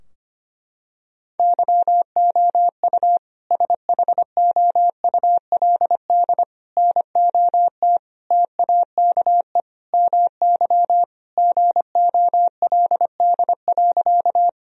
Morse code, heard as a keyed tone, about 25 wpm, 700 Hz. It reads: YOU SHOULD NOT TAKE MY GOLD.
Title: Sentences from Top 1000 Words - Review Part 3 - 25wpm